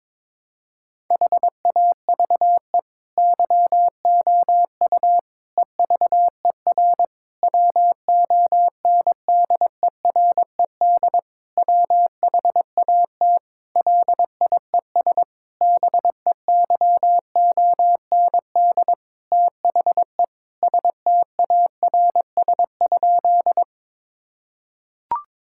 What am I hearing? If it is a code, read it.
Morse code, heard as a keyed tone, about 22 wpm, 700 Hz. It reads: HAVE YOU EVER WONDERED WHAT LIES BEYOND THE STARS?
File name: Difficult Sentences with Hints - 22wpm